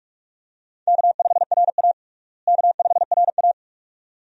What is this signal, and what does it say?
Morse code, heard as a keyed tone, about 42 wpm, 700 Hz. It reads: K5RA K5RA
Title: USA Call Signs - 4 Characters - Repeated 2x - 45wpm